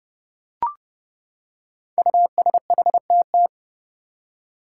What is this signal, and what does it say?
Morse code, heard as a keyed tone, about 30 wpm, 700 Hz. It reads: USHTT
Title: Five Challenging Letter-Number Combinations 30wpm